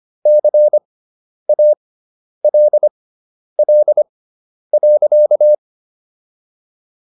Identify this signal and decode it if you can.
Morse code, heard as a keyed tone, about 25 wpm, spaced as for 10 wpm, 600 Hz. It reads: CALL.